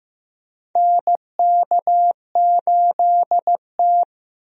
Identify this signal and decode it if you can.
Morse code, heard as a keyed tone, about 15 wpm, 700 Hz. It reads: NK8T
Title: USA Call Signs - 4 Characters - Repeated 3x - 15wpm